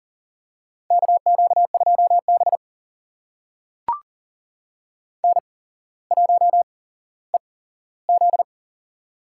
Morse code, heard as a keyed tone, about 38 wpm, spaced as for 17 wpm, 700 Hz. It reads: KQ2B N1EZ
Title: USA Call Signs - 4 Characters - 8x Character Spacing - 40wpm